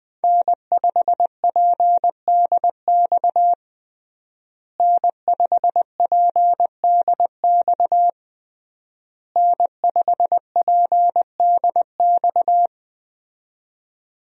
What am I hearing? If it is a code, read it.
Morse code, heard as a keyed tone, about 20 wpm, 700 Hz. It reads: N5PDX N5PDX N5PDX
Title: USA Call Signs - 5 Characters - Repeated 3x - 20wpm